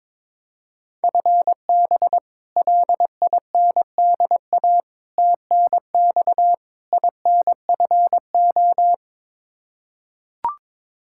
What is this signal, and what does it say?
Morse code, heard as a keyed tone, about 22 wpm, 700 Hz. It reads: FB LINDA TNX INFO
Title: Practice QSO - Exchange 3 - More Info - 22wpm